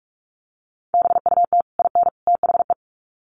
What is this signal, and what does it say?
Morse code, heard as a keyed tone, about 44 wpm, 700 Hz. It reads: BUT IN THE
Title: Sets of 3 Words 45wpm